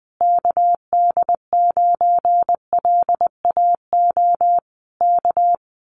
Morse code, heard as a keyed tone, about 20 wpm, 700 Hz. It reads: KD9LAO K
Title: Common QSO Phrases Encore 20wpm